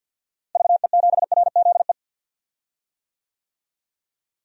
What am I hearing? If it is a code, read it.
Morse code, heard as a keyed tone, about 52 wpm, 700 Hz. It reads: VE7RZE
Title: DXCC-100 Call Signs - 6 Characters - Repeated 3x - 50wpm